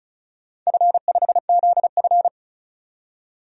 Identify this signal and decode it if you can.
Morse code, heard as a keyed tone, about 36 wpm, 700 Hz. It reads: F5ZF
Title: DXCC-100 Call Signs - 4 Characters - 35wpm